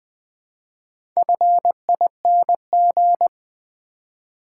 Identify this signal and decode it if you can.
Morse code, heard as a keyed tone, about 20 wpm, 700 Hz. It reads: FING
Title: Build A Word - 2 Letters At A Time - 6-Letter Words - 20wpm